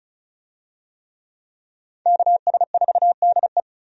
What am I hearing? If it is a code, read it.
Morse code, heard as a keyed tone, about 36 wpm, 700 Hz. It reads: KS4DE
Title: USA Call Signs - 5 Characters - 35wpm